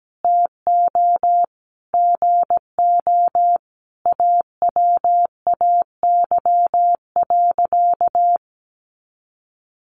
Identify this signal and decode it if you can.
Morse code, heard as a keyed tone, about 17 wpm, 700 Hz. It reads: TO GO AWAY.
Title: Sentences from Top 1500 Words 17wpm